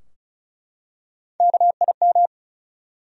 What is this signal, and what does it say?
Morse code, heard as a keyed tone, about 36 wpm, 700 Hz. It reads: KIM
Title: USA Common Names - Max 4 Letters - 35wpm